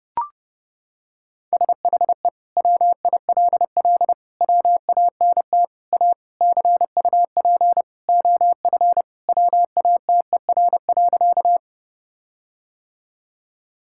Morse code, heard as a keyed tone, about 32 wpm, 700 Hz. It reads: SHE WILL WANT A CUP OF WATER.
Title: Sentences from Top 1600 Words 30wpm